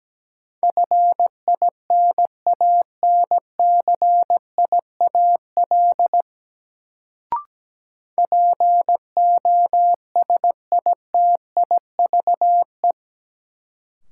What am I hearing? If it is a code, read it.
Morse code, heard as a keyed tone, about 17 wpm, 700 Hz. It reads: FINANCIAL POSITIVE
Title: Sentences from Top 1000 Words - Review Part 5 - New Word Drill - 17wpm